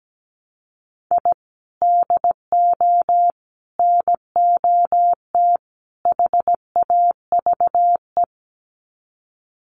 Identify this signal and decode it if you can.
Morse code, heard as a keyed tone, about 17 wpm, 700 Hz. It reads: I DO NOT HAVE
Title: Sets of 4 Words 17wpm